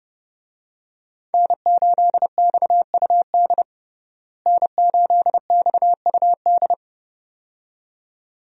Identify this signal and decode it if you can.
Morse code, heard as a keyed tone, about 30 wpm, 700 Hz. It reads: N8XUD N8XUD
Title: USA Call Signs - 5 Characters - Repeated 2x - 30wpm